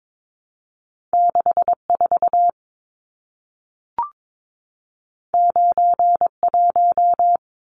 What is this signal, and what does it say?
Morse code, heard as a keyed tone, about 22 wpm, 700 Hz. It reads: 64 91
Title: Numbers - 2-Digit 22wpm